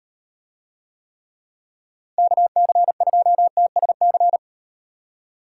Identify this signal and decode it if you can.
Morse code, heard as a keyed tone, about 38 wpm, 700 Hz. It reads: KC2TSC